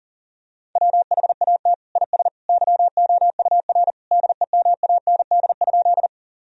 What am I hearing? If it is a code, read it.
Morse code, heard as a keyed tone, about 37 wpm, 700 Hz. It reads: WHAT IS YOUR DEMAND?
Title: Sentences from Top 1000 Words - Review Part 4 - 40wpm